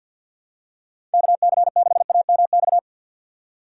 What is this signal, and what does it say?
Morse code, heard as a keyed tone, about 52 wpm, 700 Hz. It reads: KC6AMX
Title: USA Call Signs - 6 Characters - 4x Character Spacing - 50wpm